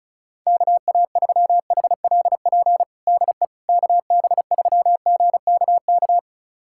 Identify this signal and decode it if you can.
Morse code, heard as a keyed tone, about 34 wpm, 700 Hz. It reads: KA3HLP DE KB3GKK
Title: USA Callsign DE Callsign 35wpm